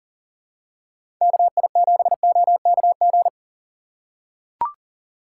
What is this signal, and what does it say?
Morse code, heard as a keyed tone, about 40 wpm, 700 Hz. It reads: KI7OKG
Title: USA Call Signs - 6 Characters - 4x Character Spacing - 40wpm